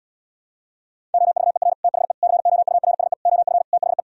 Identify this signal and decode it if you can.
Morse code, heard as a keyed tone, about 78 wpm, 700 Hz. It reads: OVER THE COURSE OF THE